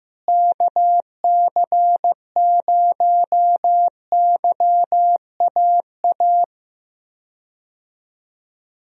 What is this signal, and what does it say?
Morse code, heard as a keyed tone, about 15 wpm, 700 Hz. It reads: KC0YAA
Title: USA Call Signs - 6 Characters - Repeated 3x - 15wpm